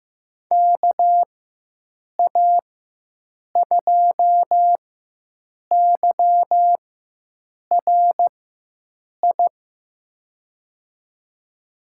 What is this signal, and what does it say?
Morse code, heard as a keyed tone, about 15 wpm, spaced as for 7 wpm, 700 Hz. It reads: KA2YRI
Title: USA Call Signs - 6 Characters - 4x Character Spacing - 15wpm